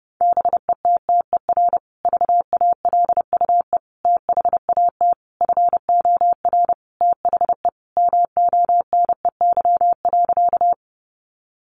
Morse code, heard as a keyed tone, about 30 wpm, 700 Hz. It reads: BETTER VALUE THAT FOR THE MONEY.
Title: Sentences from Top 600 Words 30wpm